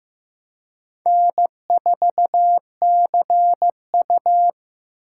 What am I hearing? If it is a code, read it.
Morse code, heard as a keyed tone, about 16 wpm, 700 Hz. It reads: N4CU
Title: USA Call Signs - 4 Characters - 2x Character Spacing - 15wpm